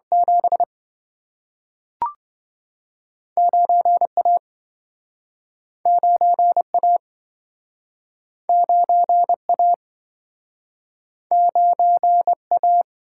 Morse code, heard as a keyed tone, about 27 wpm, 700 Hz. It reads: 7 9A 9A 9A 9A